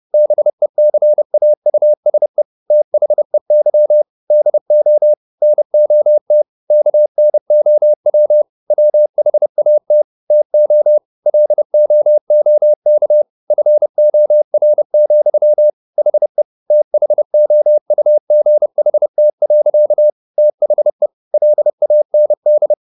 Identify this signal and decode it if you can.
Morse code, heard as a keyed tone, about 30 wpm, 600 Hz. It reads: BECAUSE THEY DO NOT KNOW WHAT TO LOOK FOR, HE THOUGHT. THE LAND
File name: the old man and the sea / the old man and the sea 1 30wpm